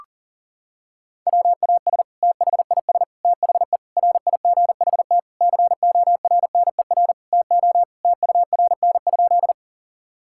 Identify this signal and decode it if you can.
Morse code, heard as a keyed tone, about 40 wpm, 700 Hz. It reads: WAS THIS THE RIGHT CORNER TO TURN?
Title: Sentences from Top 1600 Words 40wpm